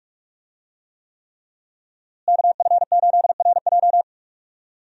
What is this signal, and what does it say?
Morse code, heard as a keyed tone, about 43 wpm, 700 Hz. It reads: KF8RJ
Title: USA Call Signs - 5 Characters - Repeated 3x - 45wpm